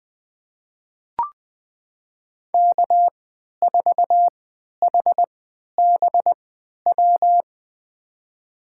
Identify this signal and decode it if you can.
Morse code, heard as a keyed tone, about 21 wpm, spaced as for 12 wpm, 700 Hz. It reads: K4HBW
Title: USA Call Signs - 5 Characters - 3x Character Spacing - 20wpm